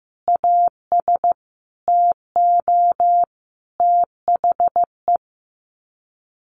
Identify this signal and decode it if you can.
Morse code, heard as a keyed tone, about 15 wpm, 700 Hz. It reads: AS TO THE
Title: Sets of 5 Words Encore2 15wpm